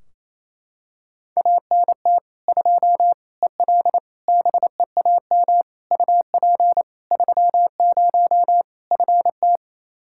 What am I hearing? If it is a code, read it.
Morse code, heard as a keyed tone, about 27 wpm, 700 Hz. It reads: ANT 2 EL BEAM UP 30 FT
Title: Practice QSO - Exchange 2 - Info - 28wpm